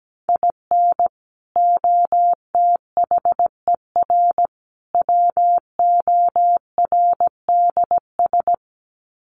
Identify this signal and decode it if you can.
Morse code, heard as a keyed tone, about 17 wpm, 700 Hz. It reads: IN OTHER WORDS
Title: Sets of 3 Words 17wpm